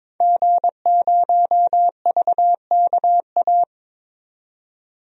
Morse code, heard as a keyed tone, about 21 wpm, 700 Hz. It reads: G0VKA
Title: DXCC-100 Call Signs - 5 Characters - 22wpm